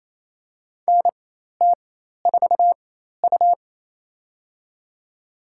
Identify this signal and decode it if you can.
Morse code, heard as a keyed tone, about 28 wpm, spaced as for 13 wpm, 700 Hz. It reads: NT4U
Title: USA Call Signs - 4 Characters - 4x Character Spacing - 28wpm